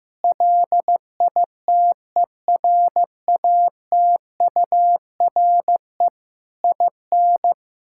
Morse code, heard as a keyed tone, about 16 wpm, 700 Hz. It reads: LITERATURE IN
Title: Sentences from Top 1000 Words - Review Part 6 - 15wpm